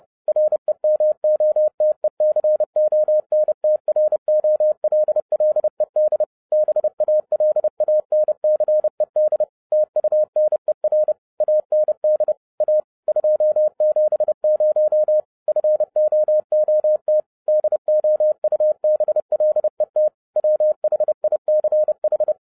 Morse code, heard as a keyed tone, about 30 wpm, 610 Hz. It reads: REMOTECONTROLLED BALANCED TUNER AND A 270 FOOT DOUBLET WHICH
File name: 2011-07-mp3 / n20 GREETINGS FROM SINGAPORE 30wpm